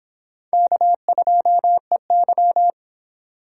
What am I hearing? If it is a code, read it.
Morse code, heard as a keyed tone, about 29 wpm, 700 Hz. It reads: K2EY